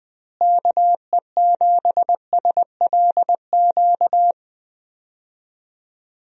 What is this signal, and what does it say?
Morse code, heard as a keyed tone, about 20 wpm, 700 Hz. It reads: KE7SLQ
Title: USA Call Signs - 6 Characters - Repeated 3x - 20wpm